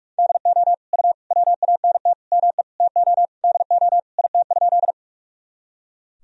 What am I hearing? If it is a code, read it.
Morse code, heard as a keyed tone, about 46 wpm, 700 Hz. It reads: DO U WANT ME TO DO IT?
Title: Sentences from Top 200 Words and Abbreviations 45wpm